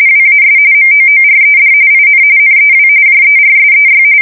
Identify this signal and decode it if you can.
RTTY, about 45.45 baud. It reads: WAS MOORED IN NEW JERSEY
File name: fsk170 / USS-New-Jersey